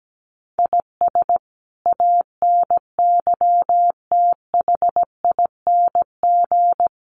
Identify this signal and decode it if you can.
Morse code, heard as a keyed tone, about 17 wpm, 700 Hz. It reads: IS ANYTHING
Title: Sentences from Top 1000 Words 17wpm